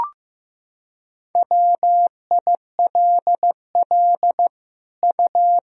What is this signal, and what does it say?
Morse code, heard as a keyed tone, about 15 wpm, 700 Hz. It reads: WILL U
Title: Sentences from Top 100 Words and Abbreviations 15wpm